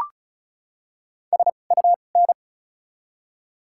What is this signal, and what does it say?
Morse code, heard as a keyed tone, about 34 wpm, spaced as for 25 wpm, 700 Hz. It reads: SUN
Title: 2x Character Spacing - Top 300 Words - 35wpm